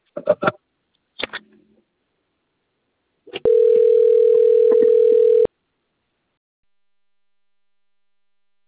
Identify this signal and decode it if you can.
Ringback tone (North America), 440 + 480 Hz.
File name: rg-1600-unknown- - - .55136